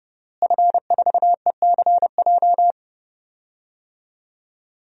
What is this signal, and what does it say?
Morse code, heard as a keyed tone, about 30 wpm, 700 Hz. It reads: F4ECJ
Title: DXCC-100 Call Signs - 5 Characters - Repeated 2x - 30wpm